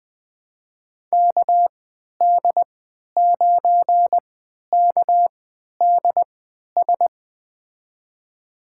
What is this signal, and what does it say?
Morse code, heard as a keyed tone, about 20 wpm, spaced as for 11 wpm, 700 Hz. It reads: KD9KDS
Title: USA Call Signs - 6 Characters - 3x Character Spacing - 20wpm